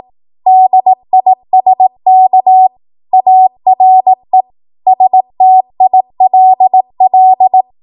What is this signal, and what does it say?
Morse code, heard as a keyed tone, about 18 wpm, 750 Hz. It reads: DISK ARE STILL
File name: Morse / 18WPM